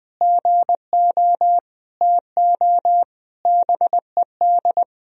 Morse code, heard as a keyed tone, about 21 wpm, 700 Hz. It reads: GO TO BED
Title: Sentences from Top 1000 Words - Review Part 4 - 20wpm